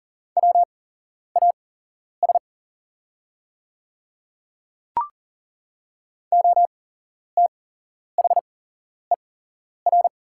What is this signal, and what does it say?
Morse code, heard as a keyed tone, about 40 wpm, spaced as for 11 wpm, 700 Hz. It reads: WAS OTHER
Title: 8x Character Spacing - Top 100 Words - 40wpm